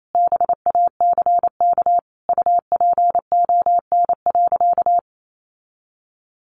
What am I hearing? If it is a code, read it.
Morse code, heard as a keyed tone, about 28 wpm, 700 Hz. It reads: BACK UPON.
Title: Sentences from Top 1100 Words 28wpm